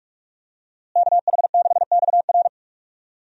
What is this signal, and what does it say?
Morse code, heard as a keyed tone, about 44 wpm, 700 Hz. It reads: KH6XR
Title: DXCC-100 Call Signs - 5 Characters - Repeated 3x - 45wpm